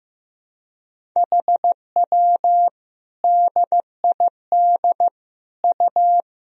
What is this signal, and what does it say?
Morse code, heard as a keyed tone, about 15 wpm, 700 Hz. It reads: HW DID U
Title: Sentences from Top 100 Words and Abbreviations 15wpm